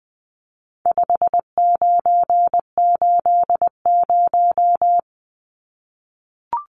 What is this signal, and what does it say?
Morse code, heard as a keyed tone, about 20 wpm, 700 Hz. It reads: 5980
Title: Numbers - 4-Digit 20wpm